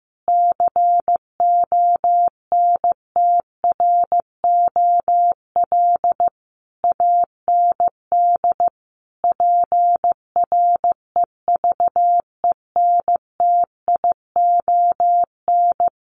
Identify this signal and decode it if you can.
Morse code, heard as a keyed tone, about 15 wpm, 700 Hz. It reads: CONTROL AND PREVENTION
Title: Sets of 5 Words Encore2 15wpm